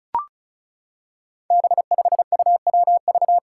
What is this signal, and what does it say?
Morse code, heard as a keyed tone, about 34 wpm, 700 Hz. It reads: B5UWV
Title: Five Challenging Letter-Number Combinations 35wpm